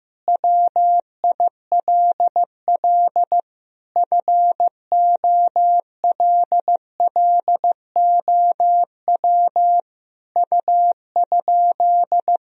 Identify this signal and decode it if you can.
Morse code, heard as a keyed tone, about 15 wpm, 700 Hz. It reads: WILL FOLLOW U?
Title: Sentences from Top 200 Words and Abbreviations 15wpm